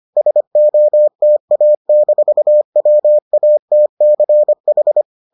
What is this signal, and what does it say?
Morse code, heard as a keyed tone, about 25 wpm, 600 Hz. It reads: SOTA-WATCH